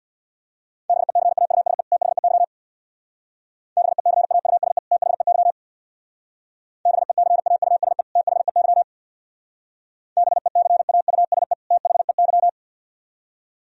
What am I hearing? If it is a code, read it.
Morse code, heard as a keyed tone, about 61 wpm, 690 Hz. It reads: BECAUSE THEY BECAUSE THEY BECAUSE THEY BECAUSE THEY
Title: Speed-Racing - Sets of 2 Words 50wpm